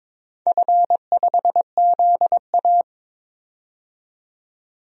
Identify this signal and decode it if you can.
Morse code, heard as a keyed tone, about 22 wpm, 700 Hz. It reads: F5ZA
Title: DXCC-100 Call Signs - 4 Characters - 22wpm